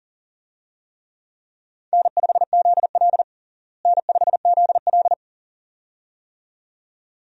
Morse code, heard as a keyed tone, about 40 wpm, 700 Hz. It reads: N5ZL N5ZL
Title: USA Call Signs - 4 Characters - Repeated 2x - 40wpm